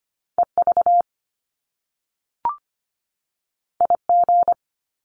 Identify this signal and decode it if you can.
Morse code, heard as a keyed tone, about 25 wpm, 700 Hz. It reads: EV IG
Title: Top Two Letter Combinations in Words 25wpm